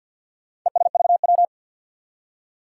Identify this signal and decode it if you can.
Morse code, heard as a keyed tone, about 52 wpm, 700 Hz. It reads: ESVW
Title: Four Challenging Letter-Number Combinations 50wpm